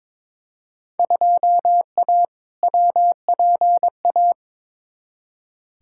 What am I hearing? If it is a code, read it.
Morse code, heard as a keyed tone, about 22 wpm, 700 Hz. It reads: 2A WPA
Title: Field Day Exchanges - 22wpm